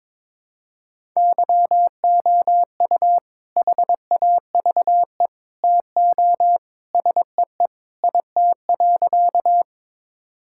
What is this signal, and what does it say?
Morse code, heard as a keyed tone, about 22 wpm, 700 Hz. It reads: YOU HAVE TO SEE IT.
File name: practice / 22wpm / Sentences with 2-Word Phrases - 22wpm